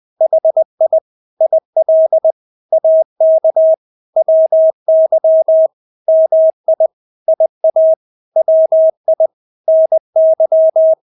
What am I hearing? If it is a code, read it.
Morse code, heard as a keyed tone, about 20 wpm, 640 Hz. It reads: HI IL AK WY MI IA WI NY